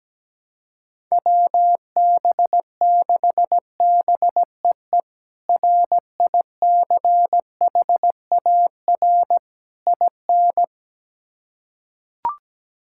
Morse code, heard as a keyed tone, about 17 wpm, 700 Hz. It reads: WB6BEE RICHAR IN
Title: K1USN SST QSO Practice - 17wpm